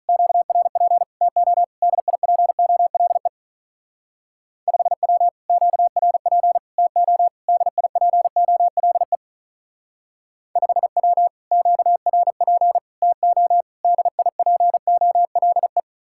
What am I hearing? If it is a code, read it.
Morse code, heard as a keyed tone, about 43 wpm, 700 Hz. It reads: QRP TO DIPOLE 5W QRP TO DIPOLE 5W QRP TO DIPOLE
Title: Speed-Racing Common QSO Phrases - 35wpm